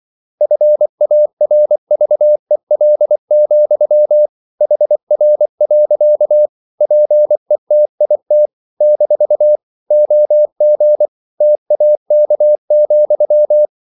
Morse code, heard as a keyed tone, about 24 wpm, 600 Hz. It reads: FARVEL, HR. PETIT - OG TAK,
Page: Ferie 24 wpm